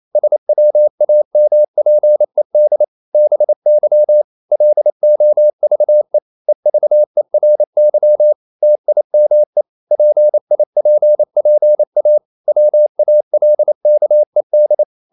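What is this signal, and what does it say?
Morse code, heard as a keyed tone, about 28 wpm, 600 Hz. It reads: SWAMPED BY LOVE EVERY TIME PIPPA WALKED
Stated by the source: Morse code practice at 28 wpm, text of Ken Follett's novel Never in English, from CW med Gnister.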